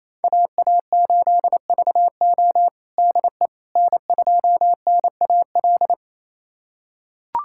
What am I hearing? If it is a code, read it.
Morse code, heard as a keyed tone, about 28 wpm, 700 Hz. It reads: AA8VO DE N2NAL
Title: USA Callsign DE Callsign - 5 Characters - 28wpm